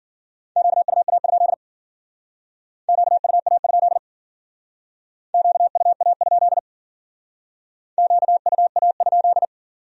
Morse code, heard as a keyed tone, about 48 wpm, 700 Hz. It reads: QUA? QUA? QUA? QUA?
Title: Speed-Racing Q-Codes - 40wpm